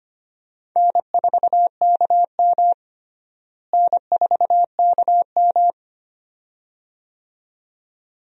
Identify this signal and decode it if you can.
Morse code, heard as a keyed tone, about 25 wpm, 700 Hz. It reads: N4KM N4KM
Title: USA Call Signs - 4 Characters - Repeated 2x - 25wpm